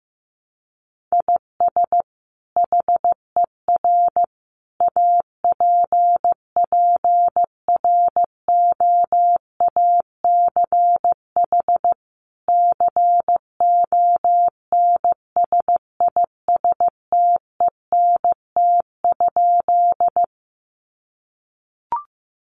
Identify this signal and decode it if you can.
Morse code, heard as a keyed tone, about 15 wpm, 700 Hz. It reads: IS HER APPROACH CONSISTENT?
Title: Sentences from Top 1500 Words 15wpm